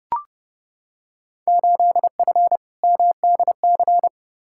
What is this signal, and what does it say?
Morse code, heard as a keyed tone, about 31 wpm, 700 Hz. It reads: 8F MDC
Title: Field Day Exchanges Encore - 30wpm